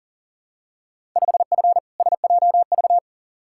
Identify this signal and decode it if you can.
Morse code, heard as a keyed tone, about 40 wpm, 700 Hz. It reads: 5F SJV
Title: Field Day Exchanges Encore - 40wpm